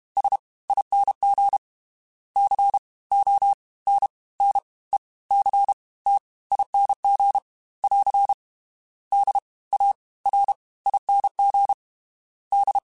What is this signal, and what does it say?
Morse code, heard as a keyed tone, about 32 wpm, 800 Hz. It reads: S ING C O N N E C T ING <AR> D A R ING D
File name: Special INGs / Special INGs-mp3s / Fast ing @30